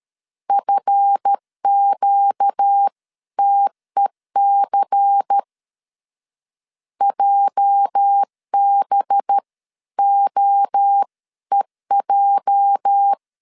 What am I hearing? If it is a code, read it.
Morse code, heard as a keyed tone, about 13 wpm, 800 Hz. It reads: FQTEC JBOEJ
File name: LEZIONI / 8 YJ45 / 8 YJ 12wpm 2235